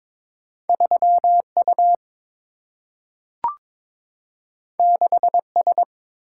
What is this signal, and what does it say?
Morse code, heard as a keyed tone, about 22 wpm, 700 Hz. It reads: 3U 6S
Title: Single-Letter Single-Number 22wpm